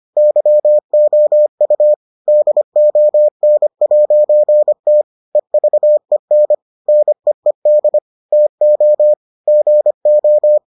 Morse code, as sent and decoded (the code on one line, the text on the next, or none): -.-- --- ..- / -.. --- -. .----. - / . ...- . -. / -. . -.. / - --- / --. ---
YOU DON'T EVEN NEED TO GO